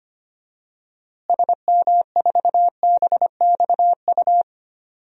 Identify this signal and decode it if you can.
Morse code, heard as a keyed tone, about 25 wpm, 700 Hz. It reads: SM4BXU
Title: DXCC-100 Call Signs - 6 Characters - Repeated 3x - 25wpm